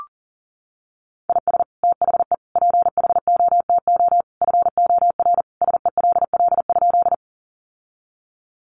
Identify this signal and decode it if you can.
Morse code, heard as a keyed tone, about 40 wpm, 700 Hz. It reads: IS THE PHOTO FOR SELL?
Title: Sentences from Top 1200 Words 40wpm